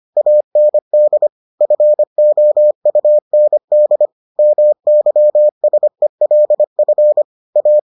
Morse code, heard as a keyed tone, about 25 wpm, 600 Hz. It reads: AND FOUND MYSELF A